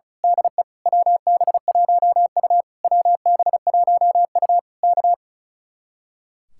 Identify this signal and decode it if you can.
Morse code, heard as a keyed tone, about 33 wpm, 700 Hz. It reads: DE WB1U WB1U K
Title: Speed-Racing Common QSO Phrases - 35wpm